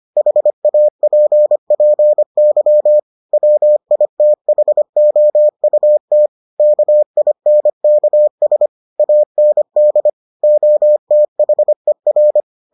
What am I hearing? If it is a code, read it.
Morse code, heard as a keyed tone, about 25 wpm, 600 Hz. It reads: HAPPY WITHOUT KINKS AND OTHER